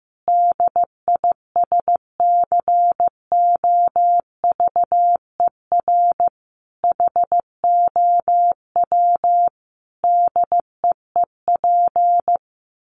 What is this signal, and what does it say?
Morse code, heard as a keyed tone, about 15 wpm, 700 Hz. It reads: DISCOVER HOW DEEP
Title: Sentences from Top 1100 Words 15wpm